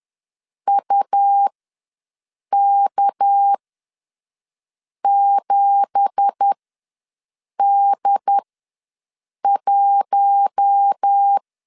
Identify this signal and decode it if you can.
Morse code, heard as a keyed tone, about 10 wpm, spaced as for 5 wpm, 800 Hz. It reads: UK7D1
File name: LEZIONI / 9 2867 ws 10wpm 4215